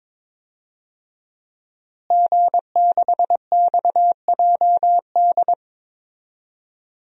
Morse code, heard as a keyed tone, about 21 wpm, 700 Hz. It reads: G6XJD
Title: DXCC-100 Call Signs - 5 Characters - Repeated 2x - 22wpm